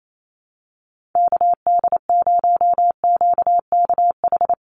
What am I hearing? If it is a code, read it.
Morse code, heard as a keyed tone, about 28 wpm, 700 Hz. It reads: KD0QKH
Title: USA Call Signs 28wpm